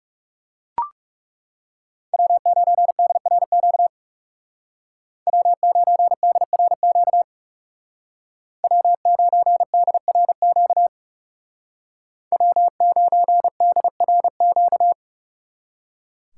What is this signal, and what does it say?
Morse code, heard as a keyed tone, about 38 wpm, 700 Hz. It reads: W9DRQ W9DRQ W9DRQ W9DRQ